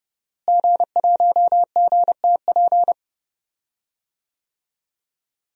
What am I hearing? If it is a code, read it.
Morse code, heard as a keyed tone, about 30 wpm, 700 Hz. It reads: G1GTP
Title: DXCC-100 Call Signs - 5 Characters - Repeated 3x - 30wpm